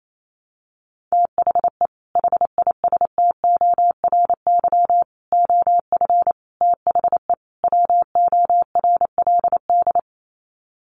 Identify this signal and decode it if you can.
Morse code, heard as a keyed tone, about 28 wpm, 700 Hz. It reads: THE HISTORY OF THE WORLD
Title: Sets of 5 Words Encore2 28wpm